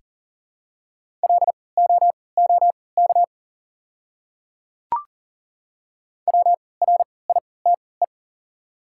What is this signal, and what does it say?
Morse code, heard as a keyed tone, about 40 wpm, spaced as for 23 wpm, 700 Hz. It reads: LOOK WRITE
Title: 3x Character Spacing - Top 200 Words - 40wpm